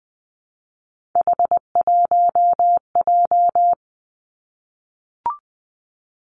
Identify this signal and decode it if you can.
Morse code, heard as a keyed tone, about 20 wpm, 700 Hz. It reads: H1J
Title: Three Challenging Letter-Number Combinations 20wpm